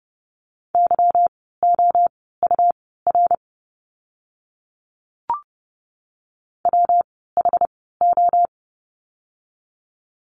Morse code, heard as a keyed tone, about 30 wpm, spaced as for 17 wpm, 700 Hz. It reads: YOUR WHO
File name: course / 30:10wpm / 237 Lesson 38f - Letter q - Words 30wpm